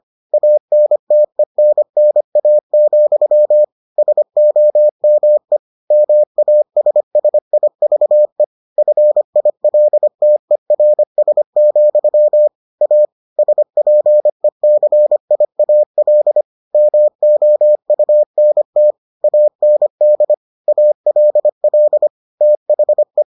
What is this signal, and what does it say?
Morse code, heard as a keyed tone, about 25 wpm, 600 Hz. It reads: ANTENNA, SOME MASSIVE FILTERS, A SPECIAL MOUNT AND ALL THE